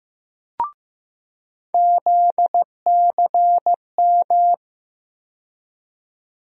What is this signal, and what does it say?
Morse code, heard as a keyed tone, about 15 wpm, 700 Hz. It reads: ZCM
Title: USA Call Sign Postfixes - 3 Letters 15wpm